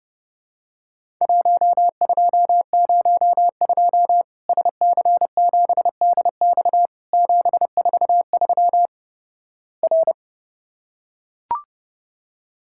Morse code, heard as a keyed tone, about 30 wpm, 700 Hz. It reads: 1202 SC7DX 743 R
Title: Worked All Europe DX Contest - QTC Exchange - 30wpm